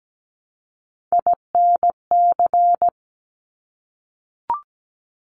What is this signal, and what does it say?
Morse code, heard as a keyed tone, about 17 wpm, 700 Hz. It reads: INC
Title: Top Three Letter Combinations in Words 17wpm